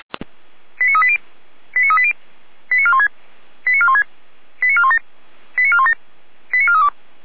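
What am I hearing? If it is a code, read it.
Five-tone selective call (ZVEI), heard as quick runs of five tones, tones of about 70 ms. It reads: 89380 89380 89416 89416 89417 89417 89432
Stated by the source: ZVEI five-tone call received via SDR.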